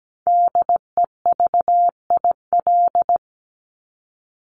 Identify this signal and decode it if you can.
Morse code, heard as a keyed tone, about 17 wpm, 700 Hz. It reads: DEVIL
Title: Top 1751-2000 Words 17wpm